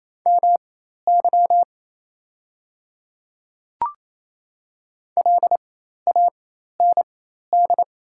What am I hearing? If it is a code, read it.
Morse code, heard as a keyed tone, about 28 wpm, spaced as for 13 wpm, 700 Hz. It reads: MY LAND
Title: 4x Character Spacing - Top 200 Words - 28wpm